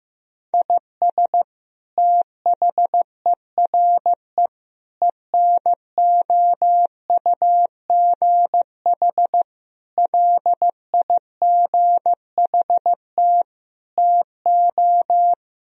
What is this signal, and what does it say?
Morse code, heard as a keyed tone, about 16 wpm, 700 Hz. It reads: IS THERE ENOUGH LIGHT TO
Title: Sentences from Top 1000 Words - Review Part 1 - 15wpm